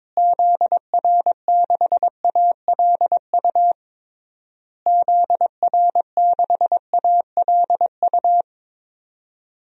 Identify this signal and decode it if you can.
Morse code, heard as a keyed tone, about 22 wpm, 700 Hz. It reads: ZR6ALU ZR6ALU
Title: DXCC-100 Call Signs - 6 Characters - Repeated 3x - 22wpm